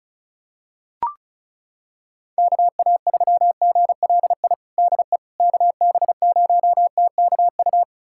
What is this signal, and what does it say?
Morse code, heard as a keyed tone, about 34 wpm, 700 Hz. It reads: KA3GLI DE KB0TKU
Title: USA Callsign DE Callsign - 6 Characters - 35wpm